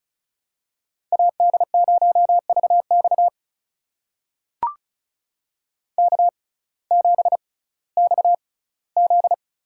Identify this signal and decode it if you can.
Morse code, heard as a keyed tone, about 34 wpm, spaced as for 19 wpm, 700 Hz. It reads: AD0VX K7XZ
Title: USA Call Signs - 5 Characters - 6x Character Spacing - 35wpm